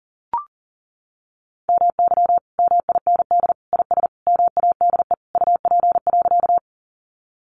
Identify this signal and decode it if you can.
Morse code, heard as a keyed tone, about 40 wpm, 700 Hz. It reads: MY MIND IS MADE UP.
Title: Sentences from Top 500 Words 40wpm